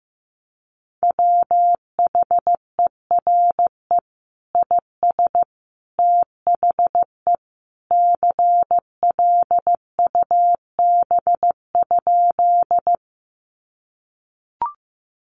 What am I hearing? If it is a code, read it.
Morse code, heard as a keyed tone, about 15 wpm, 700 Hz. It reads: WHERE IS THE CLUB?
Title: Sentences from Top 1200 Words 15wpm